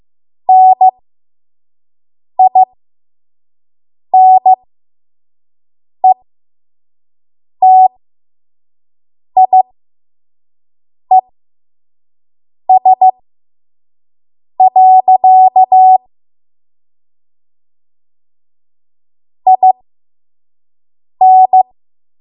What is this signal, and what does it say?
Morse code, heard as a keyed tone, about 15 wpm, spaced as for 5 wpm, 750 Hz. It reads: NINETIES. IN
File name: Morse / 05WPM